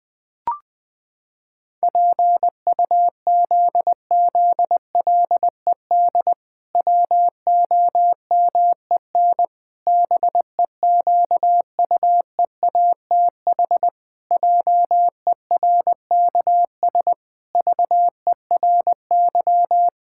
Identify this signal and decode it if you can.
Morse code, heard as a keyed tone, about 20 wpm, 700 Hz. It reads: PUZZLED WOMEN BEQUEATH JERKS VERY
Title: Pangram Challenge v2 - 20wpm